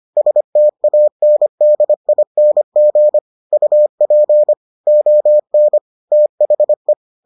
Morse code, heard as a keyed tone, about 25 wpm, 600 Hz. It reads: STANDING UP ON THE